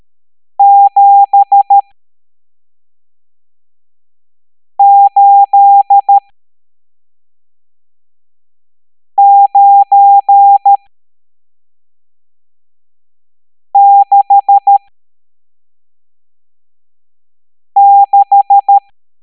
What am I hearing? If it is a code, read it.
Morse code, heard as a keyed tone, about 13 wpm, spaced as for 3 wpm, 800 Hz. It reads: 78966
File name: hamdata / L11-EX01-67890